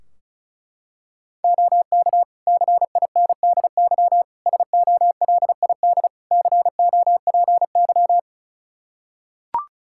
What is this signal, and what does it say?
Morse code, heard as a keyed tone, about 34 wpm, 700 Hz. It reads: OK CINDY SOLID COPY
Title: K1USN SST QSO Practice 35wpm